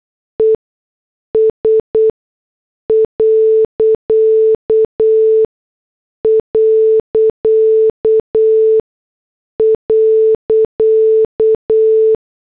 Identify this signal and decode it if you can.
Morse code, heard as a keyed tone, about 8 wpm, spaced as for 6 wpm, 440 Hz. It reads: ES...
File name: morse / headlines